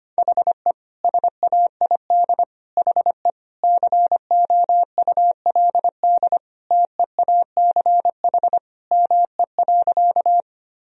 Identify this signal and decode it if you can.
Morse code, heard as a keyed tone, about 25 wpm, 700 Hz. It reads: HE SAID HE COULD TEACH ME.
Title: Sentences from Top 1000 Words - Review Part 2 - 25wpm